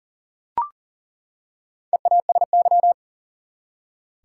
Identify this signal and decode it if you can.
Morse code, heard as a keyed tone, about 40 wpm, 700 Hz. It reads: EASY
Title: Four Letter Words - 40wpm